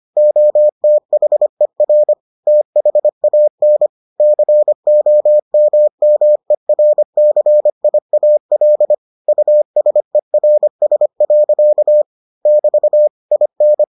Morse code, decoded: OTHER THAN COMMERCIAL USERS. <BT> IN

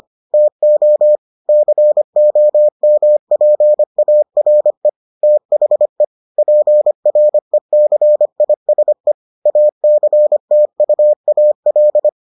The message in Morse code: TO COMPARE THE PRECISE ACTUAL

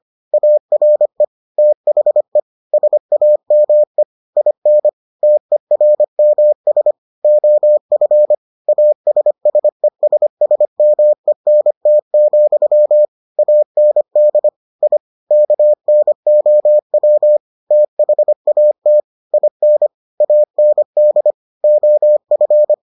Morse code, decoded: ARE THE SAME IN TERMS OF ASSESSMENT, AND I KNOW THAT IN AND OF

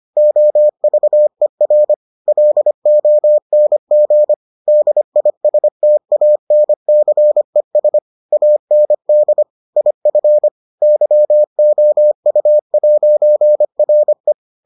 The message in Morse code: OVER LONG DISTANCES AND IF YOU'RE

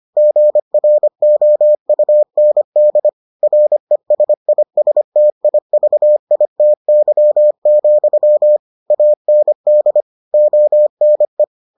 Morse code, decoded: GROUND RESISTIVITY, AND ONE

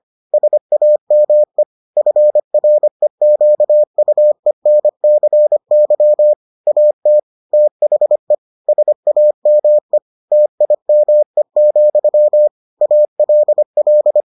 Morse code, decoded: SAME FREQUENCY AT THE SAME TIME, ALL

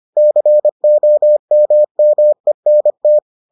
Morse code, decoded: COMMENT

